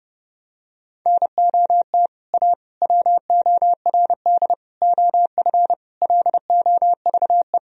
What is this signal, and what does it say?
Morse code, heard as a keyed tone, about 30 wpm, 700 Hz. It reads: NOT A WORD OF LOVE
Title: Sentences from Top 1000 Words - Review Part 1 - 30wpm